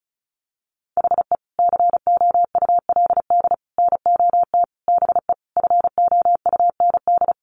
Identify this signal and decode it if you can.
Morse code, heard as a keyed tone, about 34 wpm, 700 Hz. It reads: HE COULD NOT BE FOUND
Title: Sentences from Top 1400 Words 35wpm